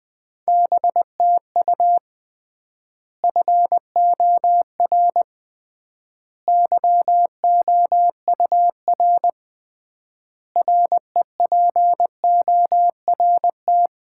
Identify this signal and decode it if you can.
Morse code, heard as a keyed tone, about 20 wpm, 700 Hz. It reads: BTU FOR YOUR REPORT
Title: Common QSO Phrases - 3x Word-Spacing - 20wpm